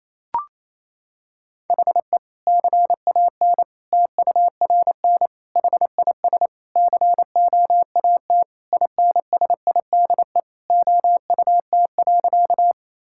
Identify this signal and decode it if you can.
Morse code, heard as a keyed tone, about 28 wpm, 700 Hz. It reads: HE CAN TURN HIS COAT INSIDE OUT.
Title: Sentences from Top 1600 Words 28wpm